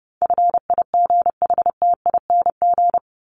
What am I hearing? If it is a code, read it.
Morse code, heard as a keyed tone, about 30 wpm, 700 Hz. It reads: FIGHTING
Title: ING Suffix - Top 100 Words with Base Words 30wpm